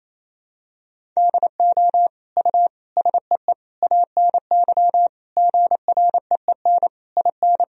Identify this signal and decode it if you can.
Morse code, heard as a keyed tone, about 28 wpm, 700 Hz. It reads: DO U SEE ANY GREEN IN
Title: Sentences from Top 500 Words and Abbreviations 28wpm